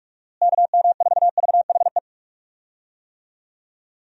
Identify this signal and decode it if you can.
Morse code, heard as a keyed tone, about 43 wpm, 700 Hz. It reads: KM4VHE
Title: USA Call Signs - 6 Characters - Repeated 2x - 45wpm